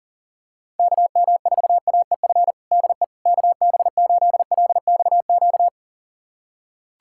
Morse code, heard as a keyed tone, about 40 wpm, 700 Hz. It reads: KM4AEF DE KB8LXQ
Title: USA Callsign DE Callsign - 6 Characters - 40wpm